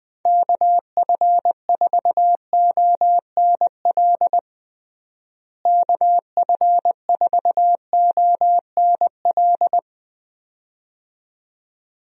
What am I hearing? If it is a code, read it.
Morse code, heard as a keyed tone, about 20 wpm, 700 Hz. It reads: KF4ONL KF4ONL
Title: USA Call Signs - 6 Characters - Repeated 2x - 20wpm